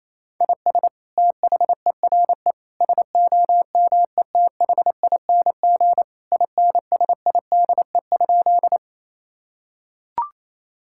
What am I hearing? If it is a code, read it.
Morse code, heard as a keyed tone, about 28 wpm, 700 Hz. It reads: IS THERE SOMETHING INSIDE?
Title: Sentences from Top 1000 Words - Review Part 6 - 28wpm